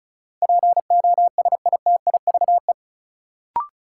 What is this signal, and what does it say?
Morse code, heard as a keyed tone, about 36 wpm, 700 Hz. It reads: POSITIVE
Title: Sentences from Top 1400 Words - New Word Drill Alt - 35wpm